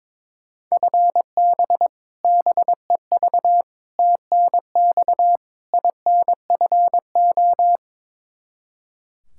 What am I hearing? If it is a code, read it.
Morse code, heard as a keyed tone, about 22 wpm, 700 Hz. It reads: FB BEV TNX INFO
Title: Practice QSO - Exchange 3 - More Info - 22wpm